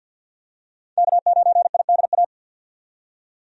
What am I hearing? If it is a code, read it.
Morse code, heard as a keyed tone, about 48 wpm, 700 Hz. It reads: K9IDA
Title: USA Call Signs - 5 Characters - 50wpm